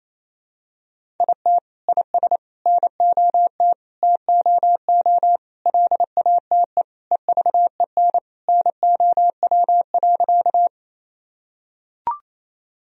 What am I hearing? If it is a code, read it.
Morse code, heard as a keyed tone, about 28 wpm, 700 Hz. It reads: IT IS NOT TOO LATE EVEN NOW.
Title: Sentences from Top 1000 Words - Review Part 1 - 28wpm